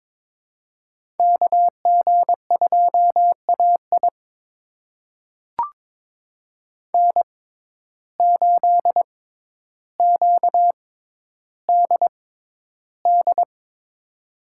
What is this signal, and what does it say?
Morse code, heard as a keyed tone, about 21 wpm, spaced as for 11 wpm, 700 Hz. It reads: KG2AI N8QDD
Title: USA Call Signs - 5 Characters - 6x Character Spacing - 22wpm